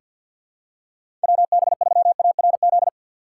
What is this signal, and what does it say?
Morse code, heard as a keyed tone, about 47 wpm, 700 Hz. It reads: WB3ARZ
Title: USA Call Signs - 6 Characters - 8x Character Spacing - 50wpm